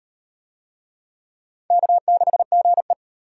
Morse code, heard as a keyed tone, about 38 wpm, 700 Hz. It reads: K6GE